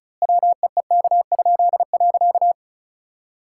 Morse code, heard as a keyed tone, about 36 wpm, 700 Hz. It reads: WEEK?.
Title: Pangram Challenge v2 - 35wpm